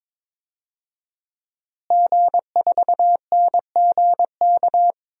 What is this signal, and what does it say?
Morse code, heard as a keyed tone, about 23 wpm, 700 Hz. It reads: G4NGK
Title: DXCC-100 Call Signs - 5 Characters - Repeated 3x - 22wpm